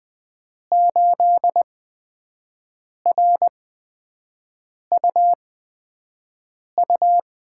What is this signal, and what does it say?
Morse code, heard as a keyed tone, about 20 wpm, spaced as for 5 wpm, 700 Hz. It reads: 8RUU
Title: USA Call Signs - 8x Character Spacing - 20wpm